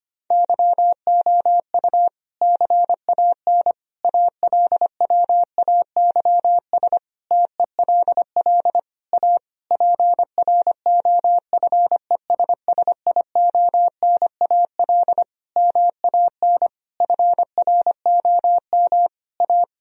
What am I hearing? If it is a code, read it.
Morse code, heard as a keyed tone, about 25 wpm, 700 Hz. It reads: YOU CAN ALWAYS TELL A PROFESSIONAL MAN FROM A
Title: Sentences from Top 1000 Words - Review Part 4 - 25wpm